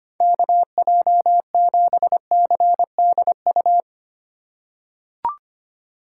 Morse code, heard as a keyed tone, about 25 wpm, 700 Hz. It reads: KJ7CDU
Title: USA Call Signs - 6 Characters - 4x Character Spacing - 25wpm